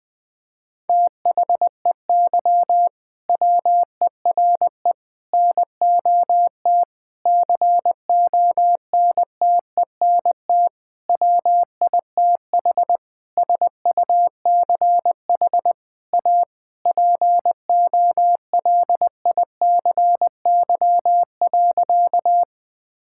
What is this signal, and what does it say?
Morse code, heard as a keyed tone, about 20 wpm, 700 Hz. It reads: THEY WERE NOT CONTENT WITH SUCH A POLICY.
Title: Sentences from Top 1000 Words - Review Part 6 - 20wpm